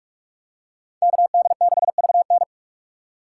Morse code, heard as a keyed tone, about 44 wpm, 700 Hz. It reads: KD6VN